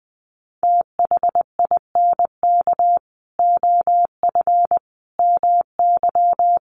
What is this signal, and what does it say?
Morse code, heard as a keyed tone, about 20 wpm, 700 Hz. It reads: THINK OF MY
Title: Sentences from Top 300 Words 20wpm